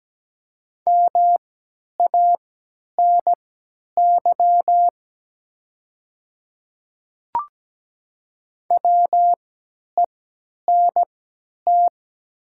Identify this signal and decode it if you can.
Morse code, heard as a keyed tone, about 17 wpm, spaced as for 10 wpm, 700 Hz. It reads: MANY WENT
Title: 3x Character Spacing - Top 200 Words - 17wpm